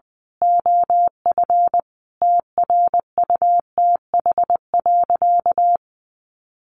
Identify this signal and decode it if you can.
Morse code, heard as a keyed tone, about 20 wpm, 700 Hz. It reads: OF TRUTH.
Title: Sentences from Top 1100 Words 20wpm